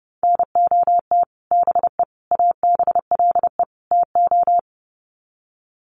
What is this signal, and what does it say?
Morse code, heard as a keyed tone, about 30 wpm, 700 Hz. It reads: NOT BE ABLE TO
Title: Sets of 4 Words 30wpm